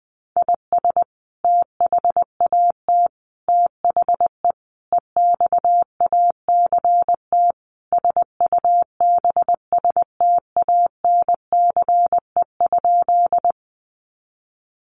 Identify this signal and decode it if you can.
Morse code, heard as a keyed tone, about 20 wpm, 700 Hz. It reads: IS THAT THE EXACT SUBSTANCE?
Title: Sentences from Top 1500 Words 20wpm